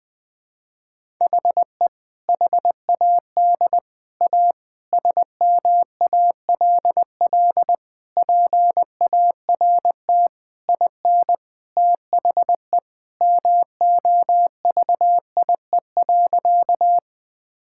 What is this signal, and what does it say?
Morse code, heard as a keyed tone, about 20 wpm, 700 Hz. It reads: HE HAD A SMALL PART IN THE MOVIE.
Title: Sentences from Top 1000 Words - Review Part 5 - 20wpm